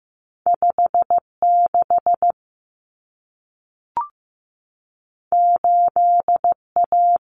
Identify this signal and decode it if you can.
Morse code, heard as a keyed tone, about 15 wpm, 700 Hz. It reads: 5 6 8 A
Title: Common Cut Numbers - 2-Digit 15wpm